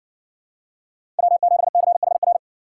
Morse code, heard as a keyed tone, about 56 wpm, 700 Hz. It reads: W7CHR